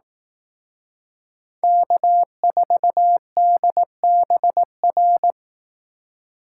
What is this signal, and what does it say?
Morse code, heard as a keyed tone, about 18 wpm, 700 Hz. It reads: K4DBR
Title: Speed Racing - USA 1x3 Call Signs 15wpm